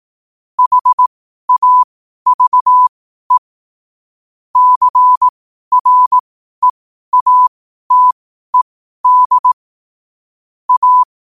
Morse code, heard as a keyed tone, about 18 wpm, spaced as for 12 wpm, 1000 Hz. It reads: HAVE CREATED A
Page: Quotes for Fri, 15 Aug 2025 in Morse Code at 12 words per minute.